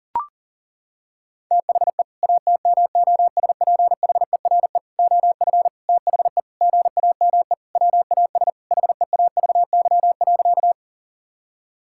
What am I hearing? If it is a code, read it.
Morse code, heard as a keyed tone, about 40 wpm, 700 Hz. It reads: THE ATMOSPHERE OF THE GAME WAS HEAVY.
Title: Sentences from Top 1600 Words 40wpm